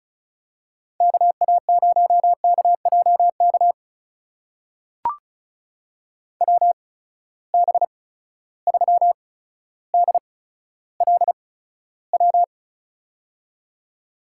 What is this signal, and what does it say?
Morse code, heard as a keyed tone, about 36 wpm, spaced as for 15 wpm, 700 Hz. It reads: KA0KJK WB3DLW
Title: USA Call Signs - 6 Characters - 8x Character Spacing - 35wpm